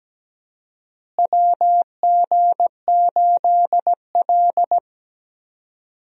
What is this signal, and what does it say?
Morse code, heard as a keyed tone, about 17 wpm, 700 Hz. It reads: WG8L
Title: USA Call Signs - 4 Characters - Repeated 3x - 17wpm